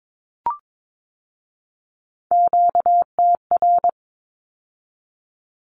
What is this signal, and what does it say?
Morse code, heard as a keyed tone, about 22 wpm, 700 Hz. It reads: QTR
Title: Q-Codes with Spoken Meaning - 22wpm